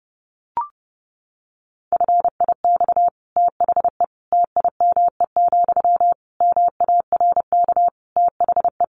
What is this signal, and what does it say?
Morse code, heard as a keyed tone, about 30 wpm, 700 Hz. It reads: FIX THE TIME, MARK THE
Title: Sentences from Top 1400 Words 30wpm